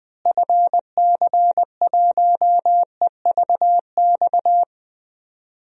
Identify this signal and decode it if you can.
Morse code, heard as a keyed tone, about 20 wpm, 700 Hz. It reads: FC1EVX
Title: DXCC-100 Call Signs - 6 Characters - Repeated 3x - 20wpm